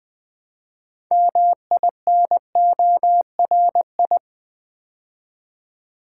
Morse code, heard as a keyed tone, about 20 wpm, 700 Hz. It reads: MINORI
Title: Build A Word - 2 Letters At A Time - 8-Letter Words - 20wpm